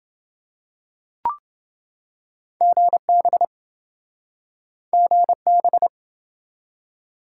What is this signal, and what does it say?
Morse code, heard as a keyed tone, about 27 wpm, 700 Hz. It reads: GB GB